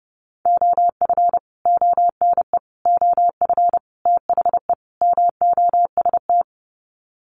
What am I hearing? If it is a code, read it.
Morse code, heard as a keyed tone, about 30 wpm, 700 Hz. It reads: OF ONE OF THE MOST
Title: Sets of 5 Words Encore2 30wpm